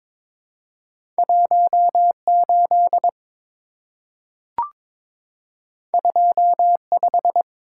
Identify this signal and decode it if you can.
Morse code, heard as a keyed tone, about 22 wpm, 700 Hz. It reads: 18 25
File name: practice / 22wpm / Two Letter-Number Combinations - 22wpm